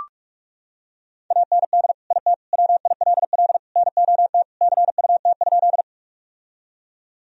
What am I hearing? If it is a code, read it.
Morse code, heard as a keyed tone, about 46 wpm, 700 Hz. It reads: AND IT WILL NOT CUT?
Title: Sentences from Top 1000 Words - Review Part 1 - 45wpm